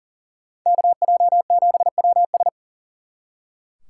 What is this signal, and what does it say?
Morse code, heard as a keyed tone, about 48 wpm, 700 Hz. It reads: KJ7WS
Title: Speed Racing - USA Call Signs 40wpm